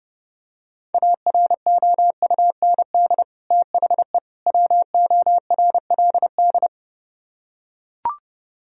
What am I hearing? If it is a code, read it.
Morse code, heard as a keyed tone, about 32 wpm, 700 Hz. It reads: AROUND THE WORLD